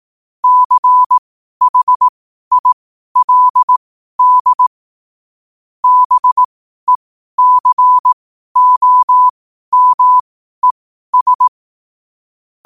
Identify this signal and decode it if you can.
Morse code, heard as a keyed tone, about 18 wpm, spaced as for 12 wpm, 1000 Hz. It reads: CHILD BECOMES
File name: qotdfiles / 12wpm